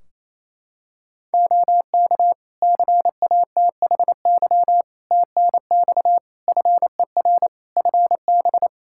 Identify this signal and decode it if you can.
Morse code, heard as a keyed tone, about 27 wpm, 700 Hz. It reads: OK CATHY TNX FER FB
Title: Practice QSO - Exchange 4 - Ending - 28wpm